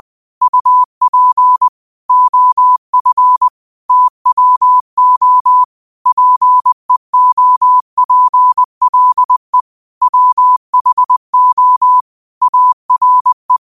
Morse code, heard as a keyed tone, about 20 wpm, 1000 Hz. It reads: UP OF TWO PEOPLE WHO ARE
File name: qotdfiles / 20wpm